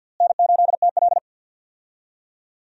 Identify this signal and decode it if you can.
Morse code, heard as a keyed tone, about 51 wpm, 700 Hz. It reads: N8TL
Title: USA Call Signs - 4 Characters - Repeated 3x - 50wpm